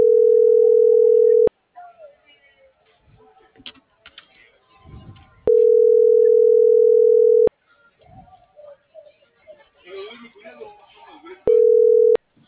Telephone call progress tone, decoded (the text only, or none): ringback tone